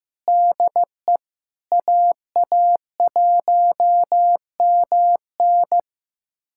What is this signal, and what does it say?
Morse code, heard as a keyed tone, about 15 wpm, 700 Hz. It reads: DE AA1MN
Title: USA Callsign DE Callsign - 5 Characters - 15wpm